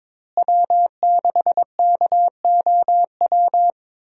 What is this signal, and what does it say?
Morse code, heard as a keyed tone, about 22 wpm, 700 Hz. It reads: W6KOW
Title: USA Call Signs - 5 Characters - Repeated 2x - 22wpm